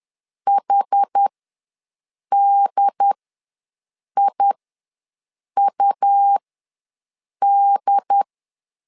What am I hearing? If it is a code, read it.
Morse code, heard as a keyed tone, about 10 wpm, spaced as for 6 wpm, 800 Hz. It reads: HDIUD
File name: LEZIONI / 2 UHDN / 2 UHD ws 10wpm 2058